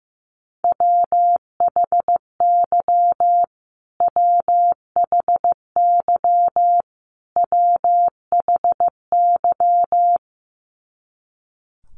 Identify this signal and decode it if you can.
Morse code, heard as a keyed tone, about 15 wpm, 700 Hz. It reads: WHY WHY WHY
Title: Top 200 Words Repeated 15wpm